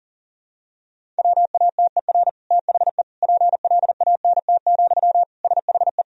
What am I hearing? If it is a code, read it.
Morse code, heard as a keyed tone, about 40 wpm, 700 Hz. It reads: WATER THE PLANT, SHE
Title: Sentences from Top 1000 Words - Review Part 1 - 40wpm